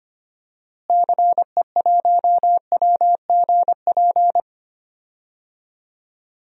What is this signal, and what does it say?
Morse code, heard as a keyed tone, about 25 wpm, 690 Hz. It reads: CE1WGP
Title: DXCC-100 Call Signs - 6 Characters - Repeated 3x - 25wpm